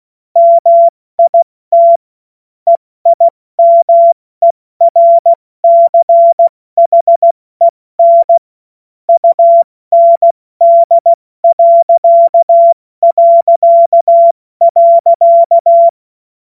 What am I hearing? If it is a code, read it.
Morse code, heard as a keyed tone, about 16 wpm, 680 Hz. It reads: MIT EIMERCHEN UND...
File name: Morsekurs / Witze / Witz 28.November 2023 70 F80